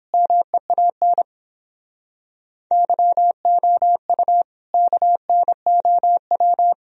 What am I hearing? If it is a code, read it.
Morse code, heard as a keyed tone, about 29 wpm, 700 Hz. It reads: MEAN YOU KNOW